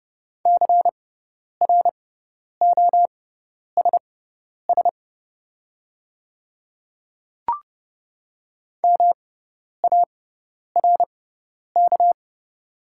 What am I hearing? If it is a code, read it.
Morse code, heard as a keyed tone, about 31 wpm, spaced as for 10 wpm, 700 Hz. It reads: CROSS MARK
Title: 6x Character Spacing - Top 300 Words - 30wpm